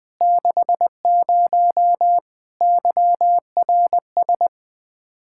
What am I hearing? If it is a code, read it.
Morse code, heard as a keyed tone, about 20 wpm, 700 Hz. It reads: 60 YRS
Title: Practice QSO - Exchange 3 - More Info - 20wpm